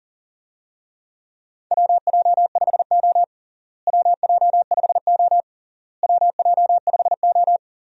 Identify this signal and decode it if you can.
Morse code, heard as a keyed tone, about 40 wpm, 700 Hz. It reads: WJ5O WJ5O WJ5O
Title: USA Call Signs - 4 Characters - Repeated 3x - 40wpm